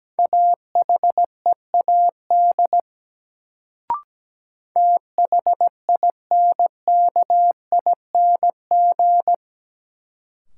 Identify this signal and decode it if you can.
Morse code, heard as a keyed tone, about 17 wpm, 700 Hz. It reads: AHEAD THINKING
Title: Sentences from Top 1000 Words - Review Part 6 - New Word Drill - 17wpm